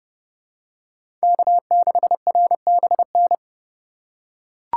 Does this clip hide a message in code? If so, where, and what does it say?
Yes: Morse code from 1.2 to 3.3 s, heard as a keyed tone, about 29 wpm, 700 Hz. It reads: K6RBN